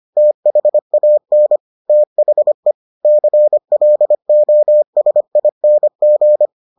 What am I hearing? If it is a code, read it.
Morse code, heard as a keyed tone, about 25 wpm, 600 Hz. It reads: THAN THE CLOSING